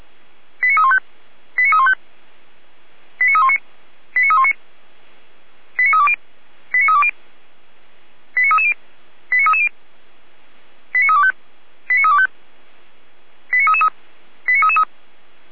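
Five-tone selective call (ZVEI), heard as quick runs of five tones, tones of about 70 ms. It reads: 89315 89315 89319 89319 89320 89320 89330 89330 89325 89325 89333 89333
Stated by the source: ZVEI five-tone call 89337 received by SDR, from the feuerwehr-orlamuende.org fire brigade server.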